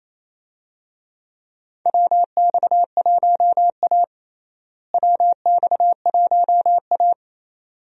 Morse code, heard as a keyed tone, about 28 wpm, 700 Hz. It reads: WX1A WX1A
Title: USA Call Signs - 4 Characters - Repeated 2x - 28wpm